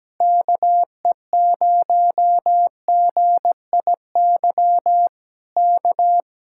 Morse code, heard as a keyed tone, about 17 wpm, 700 Hz. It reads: KE0GIY K
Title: Speed-Racing Common QSO Phrases - 17wpm